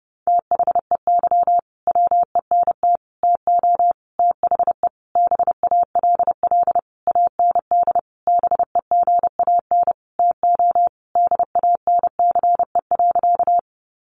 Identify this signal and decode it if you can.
Morse code, heard as a keyed tone, about 30 wpm, 700 Hz. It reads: THEY WENT TO THE BALL AND BEGAN TO DANCE.
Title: Sentences from Top 900 Words 30wpm